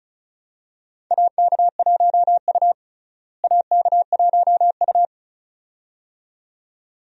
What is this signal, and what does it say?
Morse code, heard as a keyed tone, about 36 wpm, 700 Hz. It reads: AK1U AK1U
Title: USA Call Signs - 4 Characters - Repeated 2x - 35wpm